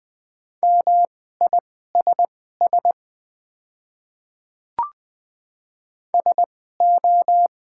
Morse code, heard as a keyed tone, about 20 wpm, spaced as for 15 wpm, 690 Hz. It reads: MISS SO
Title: 2x Character Spacing - Top 500 Words - 20wpm